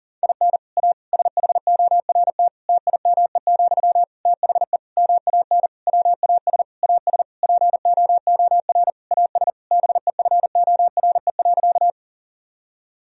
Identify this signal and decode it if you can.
Morse code, heard as a keyed tone, about 40 wpm, 700 Hz. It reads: IN A SHORT TIME, THE MAN WAS AS POOR AS BEFORE.
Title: Sentences from Top 1000 Words - Review Part 2 - 40wpm